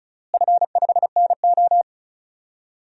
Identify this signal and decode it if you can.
Morse code, heard as a keyed tone, about 36 wpm, 700 Hz. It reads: F5NO
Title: DXCC-100 Call Signs - 4 Characters - Repeated 2x - 35wpm